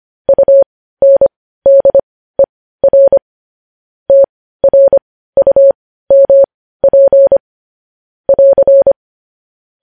Morse code, heard as a keyed tone, about 25 wpm, spaced as for 15 wpm, 550 Hz. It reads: UNDER TRUMP <AR>
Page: news headlines in morse code at 15 wpm